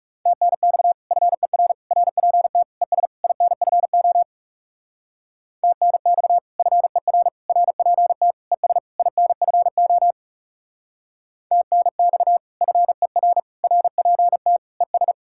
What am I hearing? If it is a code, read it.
Morse code, heard as a keyed tone, about 36 wpm, 700 Hz. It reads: TNX FER RPT ES INFO TNX FER RPT ES INFO TNX FER RPT ES
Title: Speed-Racing Common QSO Phrases - 30wpm